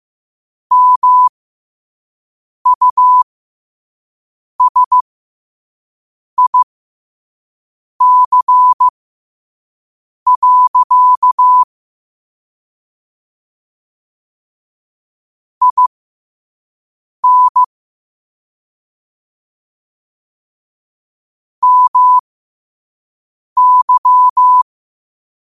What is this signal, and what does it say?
Morse code, heard as a keyed tone, about 15 wpm, spaced as for 5 wpm, 1000 Hz. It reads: MUSIC. IN MY
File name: qotdfiles / 5wpm